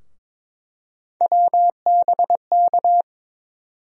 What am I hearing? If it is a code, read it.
Morse code, heard as a keyed tone, about 22 wpm, 700 Hz. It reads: WBK
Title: USA Call Sign Postfixes - 3 Letters 22wpm